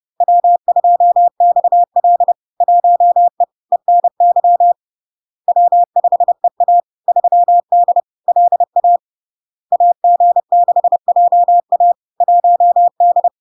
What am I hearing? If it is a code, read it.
Morse code, heard as a keyed tone, about 30 wpm, 700 Hz. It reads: W2XL 1E ENY W5EA 3D LA AG6JA 1D
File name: LARK-FD-2022-Log-in-CW-30wpm-700-Hz